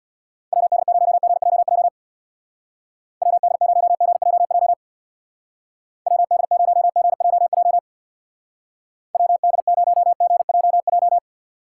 Morse code, heard as a keyed tone, about 63 wpm, 700 Hz. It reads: WD0GJJ WD0GJJ WD0GJJ WD0GJJ